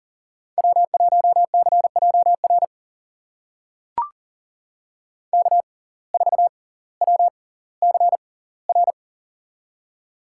Morse code, heard as a keyed tone, about 40 wpm, spaced as for 20 wpm, 700 Hz. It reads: W1CJR K4WCR
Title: USA Call Signs - 5 Characters - 6x Character Spacing - 40wpm